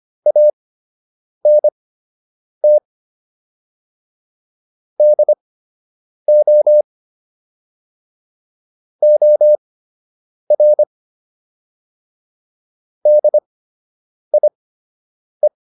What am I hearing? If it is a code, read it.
Morse code, heard as a keyed tone, about 25 wpm, spaced as for 8 wpm, 600 Hz. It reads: ANT DO OR DIE